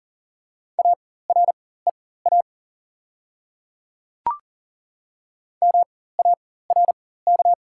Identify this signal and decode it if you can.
Morse code, heard as a keyed tone, about 40 wpm, spaced as for 19 wpm, 690 Hz. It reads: AREA MARK
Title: 4x Character Spacing - Top 500 Words - 40wpm